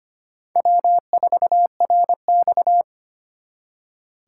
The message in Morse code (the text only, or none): W4RX